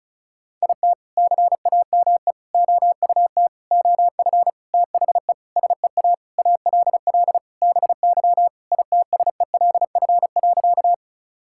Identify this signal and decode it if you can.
Morse code, heard as a keyed tone, about 34 wpm, 700 Hz. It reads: IT CAME OUT OF THE SEA ALL BY ITSELF.
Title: Sentences from Top 1000 Words - Review Part 5 - 35wpm